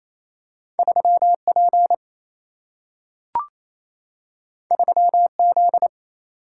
Morse code, heard as a keyed tone, about 28 wpm, 700 Hz. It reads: 3 P 3 Z
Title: Single-Letter Single-Number 28wpm